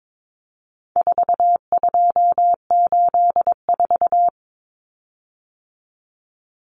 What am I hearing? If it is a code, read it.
Morse code, heard as a keyed tone, about 22 wpm, 700 Hz. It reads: 4284
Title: Common Cut Numbers - 4-Digit 22wpm